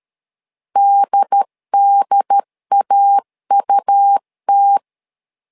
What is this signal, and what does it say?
Morse code, heard as a keyed tone, about 13 wpm, 800 Hz. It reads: DDAUT
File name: LEZIONI / 2 UHDN / 2 UHD 12wpm 1540